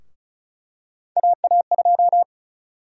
Morse code, heard as a keyed tone, about 36 wpm, 700 Hz. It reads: AA2
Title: USA Call Sign Prefixes - Repeated 2x - 35wpm